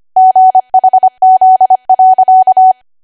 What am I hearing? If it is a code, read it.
Morse code, heard as a keyed tone, about 25 wpm, 750 Hz. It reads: GHZ.
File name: Archive / 25 WPM / 25WPM